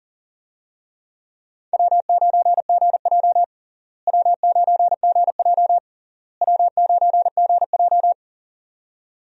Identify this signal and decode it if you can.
Morse code, heard as a keyed tone, about 40 wpm, 700 Hz. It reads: W9GJ W9GJ W9GJ
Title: USA Call Signs - 4 Characters - Repeated 3x - 40wpm